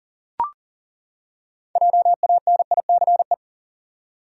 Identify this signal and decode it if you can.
Morse code, heard as a keyed tone, about 40 wpm, 700 Hz. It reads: JANICE
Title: USA Common Names - 40wpm